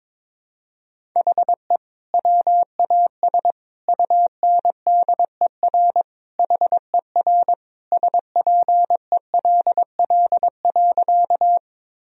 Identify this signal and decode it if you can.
Morse code, heard as a keyed tone, about 22 wpm, 700 Hz. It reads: HE WAS UNDER HER SPELL.
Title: Sentences from Top 1000 Words - Review Part 1 - 22wpm